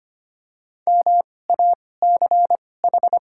Morse code, heard as a keyed tone, about 25 wpm, spaced as for 18 wpm, 700 Hz. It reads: MACH
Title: 2x Character Spacing - Top 500 Words - 25wpm